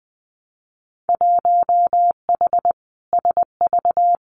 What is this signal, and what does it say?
Morse code, heard as a keyed tone, about 20 wpm, 700 Hz. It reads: 1H SV
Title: Winter Field Day Exchanges 20wpm